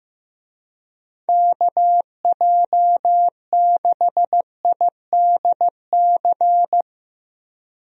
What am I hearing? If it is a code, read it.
Morse code, heard as a keyed tone, about 15 wpm, 700 Hz. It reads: KJ6IDC
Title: USA Call Signs - 6 Characters - Repeated 2x - 15wpm